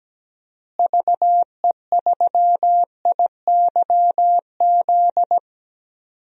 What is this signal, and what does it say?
Morse code, heard as a keyed tone, about 17 wpm, 700 Hz. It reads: VE3IYZ
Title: DXCC-100 Call Signs - 6 Characters - Repeated 3x - 17wpm